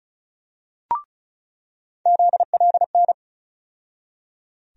Morse code, heard as a keyed tone, about 36 wpm, 700 Hz. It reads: ZLN
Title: USA Call Sign Postfixes - 3 Letters 35wpm